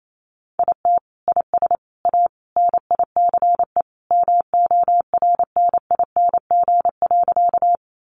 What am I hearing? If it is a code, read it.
Morse code, heard as a keyed tone, about 28 wpm, 700 Hz. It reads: IT IS A NICE MORNING.
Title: Sentences from Top 1100 Words 28wpm